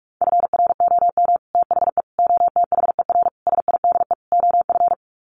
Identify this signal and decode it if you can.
Morse code, heard as a keyed tone, about 44 wpm, 700 Hz. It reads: FROM THE OTHER SIDE OF
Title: Sets of 5 Words Encore 45wpm